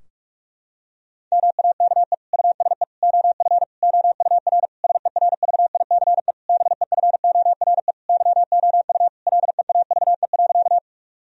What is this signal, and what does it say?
Morse code, heard as a keyed tone, about 46 wpm, 700 Hz. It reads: MAKE USE OF OUR SERVICE BEFORE YOU LEAVE.
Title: Sentences from Top 1000 Words - Review Part 4 - 45wpm